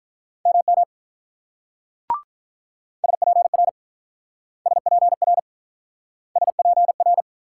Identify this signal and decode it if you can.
Morse code, heard as a keyed tone, about 43 wpm, 700 Hz. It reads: MM SPR SPR SPR
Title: Speed-Racing Multi-letter Phonemes - 35wpm